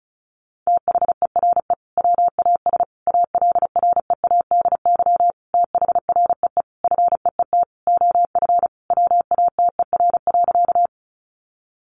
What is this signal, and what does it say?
Morse code, heard as a keyed tone, about 34 wpm, 700 Hz. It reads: THERE WAS ALREADY THREE FEET OF WATER.
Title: Sentences from Top 700 Words 35wpm